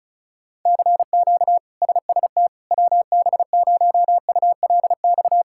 Morse code, heard as a keyed tone, about 36 wpm, 700 Hz. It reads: CQ SST WB0ULX
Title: K1USN SST QSO Practice - 35wpm